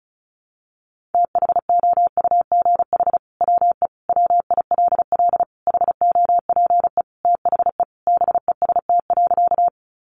Morse code, heard as a keyed tone, about 34 wpm, 700 Hz. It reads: THOUGH WE WILL HOPE THE BEST.
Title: Sentences from Top 1000 Words 35wpm